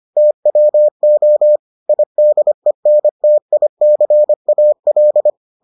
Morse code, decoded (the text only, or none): TWO IDENTICAL